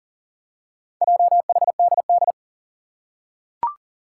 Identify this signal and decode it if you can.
Morse code, heard as a keyed tone, about 40 wpm, 700 Hz. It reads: JHDD
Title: Four Challenging Letter-Number Combinations 40wpm